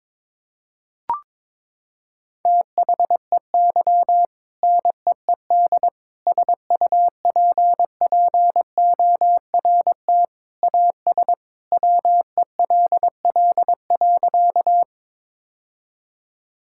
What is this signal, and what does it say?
Morse code, heard as a keyed tone, about 22 wpm, 700 Hz. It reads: THEY NEED SUPPORT AS WELL.
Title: Sentences from Top 1000 Words - Review Part 5 - 22wpm